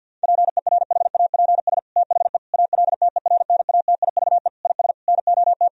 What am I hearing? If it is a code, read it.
Morse code, heard as a keyed tone, about 50 wpm, 700 Hz. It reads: PERHAPS THE ALTERNATIVE IS NOT
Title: Sentences from Top 1000 Words - Review Part 4 - 50wpm